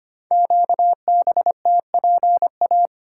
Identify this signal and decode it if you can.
Morse code, heard as a keyed tone, about 25 wpm, 700 Hz. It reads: QBTPA
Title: Code Groups - 5 Letters - 25wpm